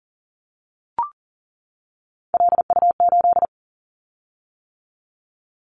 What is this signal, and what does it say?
Morse code, heard as a keyed tone, about 42 wpm, 700 Hz. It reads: LU8
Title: Callsign Prefixes for Top 100 DXCC - 40wpm